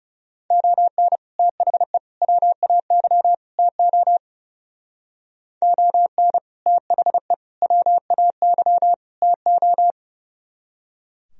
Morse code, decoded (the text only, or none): ON THE WAY TO ON THE WAY TO